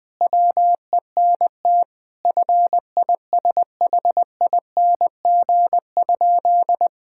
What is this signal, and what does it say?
Morse code, heard as a keyed tone, about 20 wpm, 700 Hz. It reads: WENT FISHING?
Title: Sentences from Top 1000 Words - Review Part 5 - 20wpm